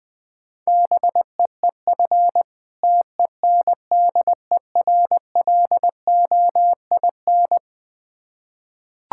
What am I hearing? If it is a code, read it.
Morse code, heard as a keyed tone, about 21 wpm, 690 Hz. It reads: BEEF TENDERLOIN
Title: Difficult Words with Hints - 20wpm